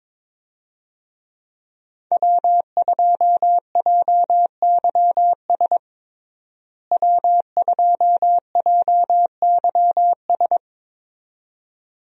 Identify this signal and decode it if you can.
Morse code, heard as a keyed tone, about 22 wpm, 700 Hz. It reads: W2JYS W2JYS
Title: USA Call Signs - 5 Characters - Repeated 2x - 22wpm